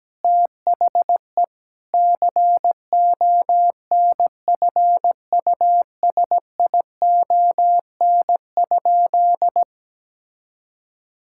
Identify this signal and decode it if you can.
Morse code, heard as a keyed tone, about 17 wpm, 700 Hz. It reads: THE CONFUSION?
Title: Sentences from Top 1600 Words 17wpm